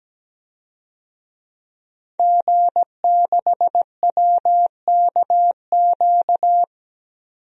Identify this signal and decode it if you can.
Morse code, heard as a keyed tone, about 17 wpm, 700 Hz. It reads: G6WKQ
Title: DXCC-100 Call Signs - 5 Characters - 17wpm